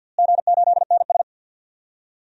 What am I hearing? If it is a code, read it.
Morse code, heard as a keyed tone, about 53 wpm, 700 Hz. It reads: G8NS
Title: DXCC-100 Call Signs - 4 Characters - Repeated 2x - 50wpm